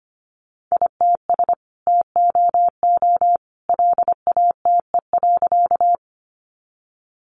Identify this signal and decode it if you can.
Morse code, heard as a keyed tone, about 25 wpm, 700 Hz. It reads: ITS TOO LATE.
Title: Sentences from Top 600 Words 25wpm